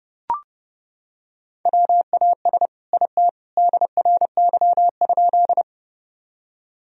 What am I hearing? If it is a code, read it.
Morse code, heard as a keyed tone, about 32 wpm, 700 Hz. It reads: WAS IT DRY?
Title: Sentences from Top 1000 Words - Review Part 3 - 30wpm